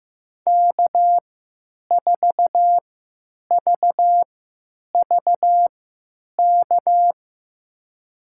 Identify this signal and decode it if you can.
Morse code, heard as a keyed tone, about 15 wpm, spaced as for 9 wpm, 700 Hz. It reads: K4VVK
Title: USA Call Signs - 5 Characters - 3x Character Spacing - 15wpm